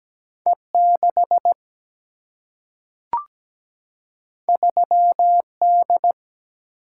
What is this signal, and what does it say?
Morse code, heard as a keyed tone, about 17 wpm, 700 Hz. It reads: E6 3D
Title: Single-Letter Single-Number 17wpm